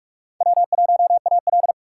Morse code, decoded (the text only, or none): W1AL